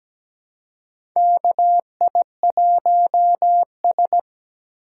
Morse code, heard as a keyed tone, about 17 wpm, 700 Hz. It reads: KI1S
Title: USA Call Signs - 3x Character Spacing - 17wpm